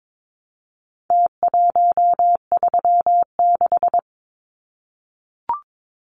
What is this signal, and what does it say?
Morse code, heard as a keyed tone, about 22 wpm, 700 Hz. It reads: T136
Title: Numbers - Contest Style - 4-Digit 22wpm